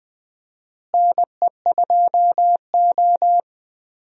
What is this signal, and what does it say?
Morse code, heard as a keyed tone, about 21 wpm, 700 Hz. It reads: NE2O
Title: USA Call Signs - 4 Characters - Repeated 3x - 20wpm